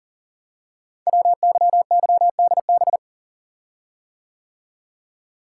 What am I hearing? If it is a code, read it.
Morse code, heard as a keyed tone, about 39 wpm, 700 Hz. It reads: WYYDB
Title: Code Groups - 5 Letters - 40wpm